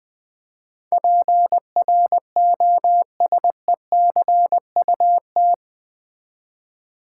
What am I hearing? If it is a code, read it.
Morse code, heard as a keyed tone, about 20 wpm, 700 Hz. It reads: PROSECUT
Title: Build A Word - 4 Letters At A Time - 11-Letter Words - 20wpm